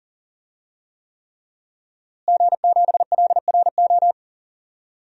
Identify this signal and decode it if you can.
Morse code, heard as a keyed tone, about 40 wpm, 700 Hz. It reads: G7LRO
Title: DXCC-100 Call Signs - 5 Characters - Repeated 3x - 40wpm